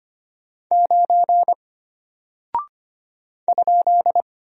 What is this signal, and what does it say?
Morse code, heard as a keyed tone, about 25 wpm, 700 Hz. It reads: 9 ?
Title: Single Character (Letter:Number:Punctuation:Prosign) Alt 25wpm